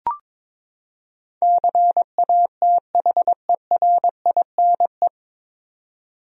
Morse code, heard as a keyed tone, about 22 wpm, 700 Hz. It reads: CATHERINE
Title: USA Common Names - 22wpm